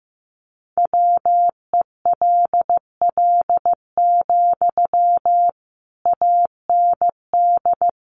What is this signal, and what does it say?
Morse code, heard as a keyed tone, about 15 wpm, 700 Hz. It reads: WELL, AND
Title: Sentences from Top 200 Words 15wpm